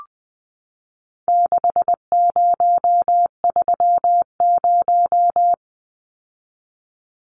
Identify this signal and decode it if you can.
Morse code, heard as a keyed tone, about 20 wpm, 700 Hz. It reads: 6030
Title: Numbers - 4-Digit 20wpm